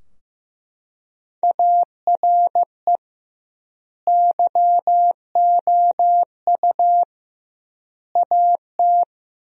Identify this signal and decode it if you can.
Morse code, heard as a keyed tone, about 15 wpm, 700 Hz. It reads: ARE YOU AT
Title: 2x Word-Spacing - Sentences from Top 200 Words - 15wpm